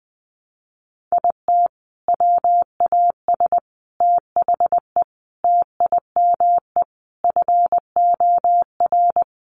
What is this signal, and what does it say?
Morse code, heard as a keyed tone, about 20 wpm, 700 Hz. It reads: IT WAS THE TIME FOR
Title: Sentences from Top 700 Words 20wpm